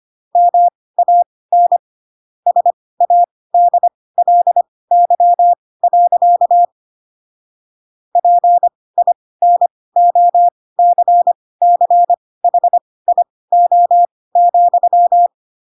Morse code, heard as a keyed tone, about 25 wpm, spaced as for 18 wpm, 700 Hz. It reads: MAN SADLY. PINOCCHIO,